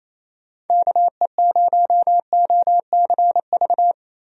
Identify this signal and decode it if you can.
Morse code, heard as a keyed tone, about 28 wpm, 700 Hz. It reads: KE0OCV
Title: USA Call Signs - 6 Characters - Repeated 2x - 28wpm